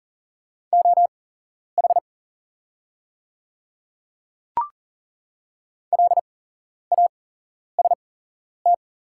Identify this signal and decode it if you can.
Morse code, heard as a keyed tone, about 40 wpm, spaced as for 11 wpm, 700 Hz. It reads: OH LAST
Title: 8x Character Spacing - Top 500 Words - 40wpm